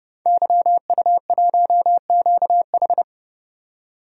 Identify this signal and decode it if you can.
Morse code, heard as a keyed tone, about 30 wpm, 700 Hz. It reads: YU1QH
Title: DXCC-100 Call Signs - 5 Characters - 30wpm